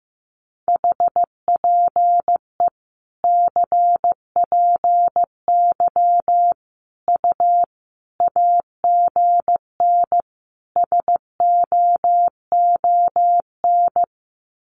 Morse code, heard as a keyed tone, about 15 wpm, 700 Hz. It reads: HPE CPY U AGN SOON
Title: Common QSO Phrases 15wpm